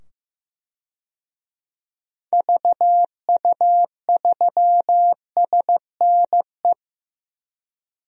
Morse code, heard as a keyed tone, about 15 wpm, 700 Hz. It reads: VU3SNE
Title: DXCC-100 Call Signs - 6 Characters - 15wpm